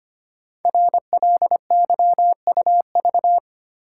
Morse code, heard as a keyed tone, about 25 wpm, 700 Hz. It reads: RLYUV
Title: Code Groups - 5 Letters - 25wpm